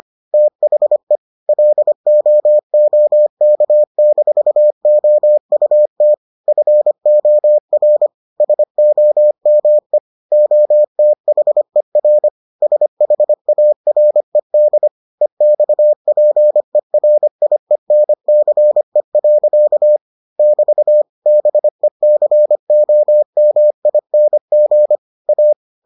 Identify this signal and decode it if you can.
Morse code, heard as a keyed tone, about 25 wpm, 600 Hz. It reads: THE LOOK-OUT FOR SOME OTHER SHARED EXPERIENCE. <BT> BECOMING A